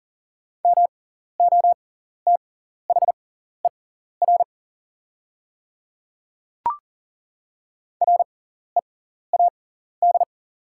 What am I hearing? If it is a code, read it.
Morse code, heard as a keyed tone, about 40 wpm, spaced as for 14 wpm, 700 Hz. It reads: MOTHER READ
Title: 6x Character Spacing - Top 200 Words - 40wpm